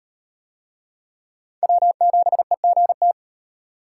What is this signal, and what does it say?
Morse code, heard as a keyed tone, about 32 wpm, 700 Hz. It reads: W7EGT